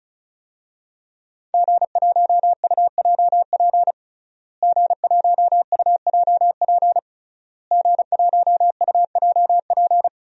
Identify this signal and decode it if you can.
Morse code, heard as a keyed tone, about 36 wpm, 700 Hz. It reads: G1UJP G1UJP G1UJP
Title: DXCC-100 Call Signs - 5 Characters - Repeated 3x - 35wpm